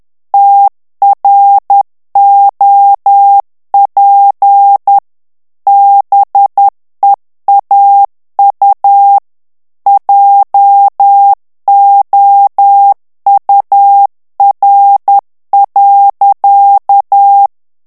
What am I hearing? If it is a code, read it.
Morse code, heard as a keyed tone, about 10 wpm, 800 Hz. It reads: TROP BEAU JOUR.
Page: Français - texte en clair 1 - vitesse : 10 mots minute